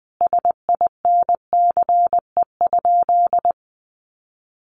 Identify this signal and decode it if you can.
Morse code, heard as a keyed tone, about 20 wpm, 700 Hz. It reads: SINCE?
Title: Sentences from Top 700 Words 20wpm